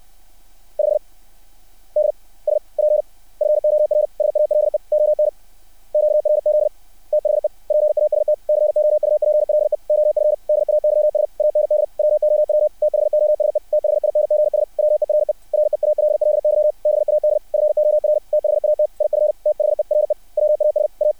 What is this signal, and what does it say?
Morse code, heard as a keyed tone, about 60 wpm, 600 Hz. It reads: 5 DSP FOR SALE ON QRZ THE PAST COUPLE OF DAYS AND WOW THOSE THINGS WERE REALLY BAD FOR THAT IF THERE WAS A
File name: archived / CRSnet